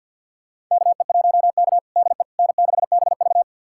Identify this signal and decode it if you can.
Morse code, heard as a keyed tone, about 52 wpm, 690 Hz. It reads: KE1K DE N6BV
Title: USA Callsign DE Callsign - 4 Characters - 50wpm